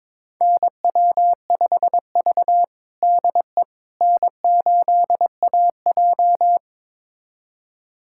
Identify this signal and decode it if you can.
Morse code, heard as a keyed tone, about 22 wpm, 700 Hz. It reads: NW5V DE N8AJ
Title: USA Callsign DE Callsign - 4 Characters - 22wpm